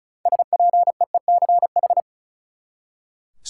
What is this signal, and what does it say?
Morse code, heard as a keyed tone, about 36 wpm, 700 Hz. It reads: SPEECH
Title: Six Letter Words - 35wpm